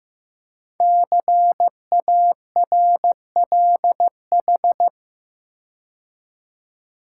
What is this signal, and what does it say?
Morse code, heard as a keyed tone, about 15 wpm, 700 Hz. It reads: CARLH
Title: Code Groups - 5 Letters - 15wpm